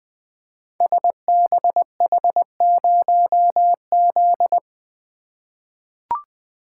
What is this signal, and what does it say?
Morse code, heard as a keyed tone, about 20 wpm, 700 Hz. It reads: SBH0Z
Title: Code Groups - 5 Letter:Numbers - 20wpm